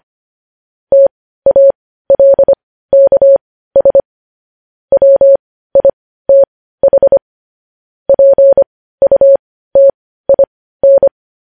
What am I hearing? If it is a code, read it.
Morse code, heard as a keyed tone, about 25 wpm, spaced as for 15 wpm, 550 Hz. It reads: TALKS WITH PUTIN
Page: news headlines in morse code at 15 wpm